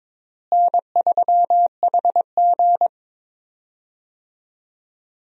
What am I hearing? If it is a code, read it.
Morse code, heard as a keyed tone, about 21 wpm, 700 Hz. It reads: N3HG
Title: USA Call Signs - 4 Characters - Repeated 3x - 22wpm